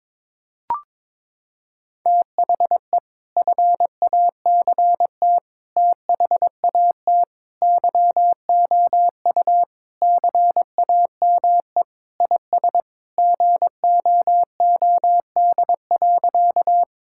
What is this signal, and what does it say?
Morse code, heard as a keyed tone, about 21 wpm, 700 Hz. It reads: THE FACT THAT YOU CAME IS GOOD.
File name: practice / 22wpm / Sentences with 3-Word Phrases - 22wpm